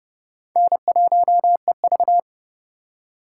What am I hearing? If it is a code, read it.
Morse code, heard as a keyed tone, about 29 wpm, 700 Hz. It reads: N1EV